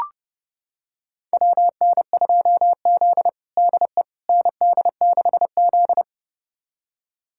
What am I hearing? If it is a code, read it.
Morse code, heard as a keyed tone, about 32 wpm, 700 Hz. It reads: WN2Z DE ND6Z
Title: USA Callsign DE Callsign - 4 Characters - 30wpm